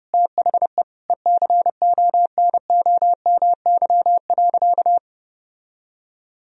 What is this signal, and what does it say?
Morse code, heard as a keyed tone, about 30 wpm, 700 Hz. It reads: THE ECONOMY.
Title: Sentences from Top 1000 Words - Review Part 3 - 30wpm